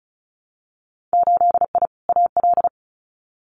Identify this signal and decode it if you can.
Morse code, heard as a keyed tone, about 36 wpm, 700 Hz. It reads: 8I AL
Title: Winter Field Day Exchanges Encore 35wpm